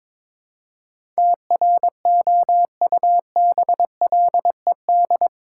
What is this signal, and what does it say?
Morse code, heard as a keyed tone, about 22 wpm, 700 Hz. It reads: TROUBLED
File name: practice / 22wpm / Eight Letter Words - 22wpm